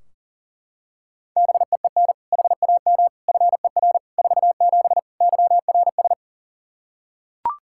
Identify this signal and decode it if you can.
Morse code, heard as a keyed tone, about 40 wpm, 700 Hz. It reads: BEEN HAM FER 47 YRS
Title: Practice QSO - Exchange 3 - More Info - 40wpm